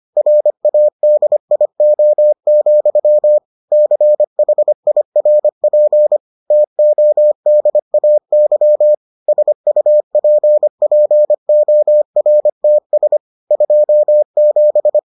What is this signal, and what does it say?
Morse code, heard as a keyed tone, about 25 wpm, 600 Hz. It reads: RADIO, CHIRP TODAY SUPPORTS 27